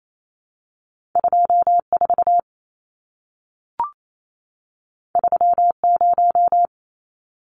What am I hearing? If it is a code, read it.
Morse code, heard as a keyed tone, about 28 wpm, 700 Hz. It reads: 24 30
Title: Numbers - 2-Digit 28wpm